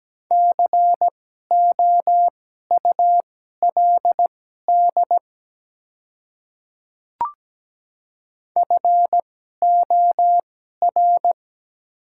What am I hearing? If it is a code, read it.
Morse code, heard as a keyed tone, about 17 wpm, spaced as for 12 wpm, 700 Hz. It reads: COULD FOR
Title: 2x Character Spacing - Top 200 Words - 17wpm